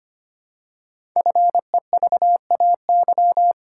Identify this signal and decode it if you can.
Morse code, heard as a keyed tone, about 25 wpm, 700 Hz. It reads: FEVAY
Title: Code Groups - 5 Letter:Numbers - 25wpm